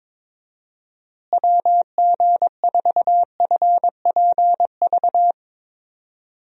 Morse code, heard as a keyed tone, about 22 wpm, 700 Hz. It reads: WG4FPV
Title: USA Call Signs - 6 Characters - Repeated 2x - 22wpm